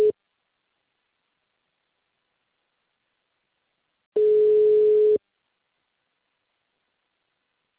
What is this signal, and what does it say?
Ringback tone (Europe, 425 Hz).